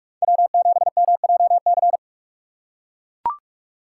Morse code, heard as a keyed tone, about 42 wpm, 700 Hz. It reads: W7MJC
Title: USA Call Signs - 2x Character Spacing - 45wpm